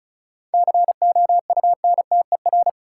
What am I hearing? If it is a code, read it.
Morse code, heard as a keyed tone, about 36 wpm, 700 Hz. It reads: COUNTER
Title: Seven Letter Words - 35wpm